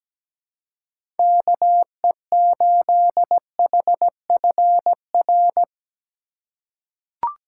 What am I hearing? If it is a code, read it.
Morse code, heard as a keyed tone, about 17 wpm, 700 Hz. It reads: KE8HFR
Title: USA Call Signs - 6 Characters - 4x Character Spacing - 17wpm